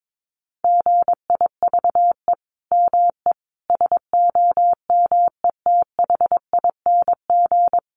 Morse code, heard as a keyed tone, about 22 wpm, 700 Hz. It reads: GIVE ME SOMETHING
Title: Sentences from Top 1500 Words 22wpm